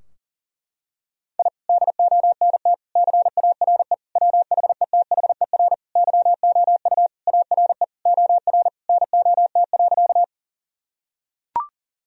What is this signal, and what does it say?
Morse code, heard as a keyed tone, about 40 wpm, 700 Hz. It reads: I DONT CARE WHETHER YOU ARE OR NOT.
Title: Sentences from Top 1000 Words - Review Part 3 - 40wpm